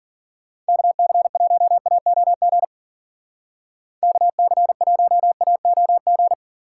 Morse code, heard as a keyed tone, about 50 wpm, 700 Hz. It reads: KC1AOG KC1AOG
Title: Speed Racing - USA 2x3 Call Signs 40wpm